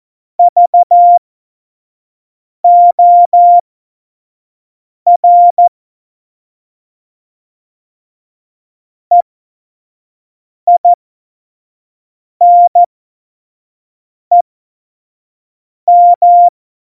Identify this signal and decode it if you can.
Morse code, heard as a keyed tone, about 14 wpm, spaced as for 5 wpm, 700 Hz. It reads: VOR EINEM
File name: Morsekurs / Witze / Witz 19. Februar 2019, 25 F70